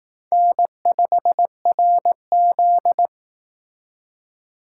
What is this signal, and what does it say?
Morse code, heard as a keyed tone, about 18 wpm, 700 Hz. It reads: N5RZ
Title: Speed Racing - USA 1x2 Call Signs 15wpm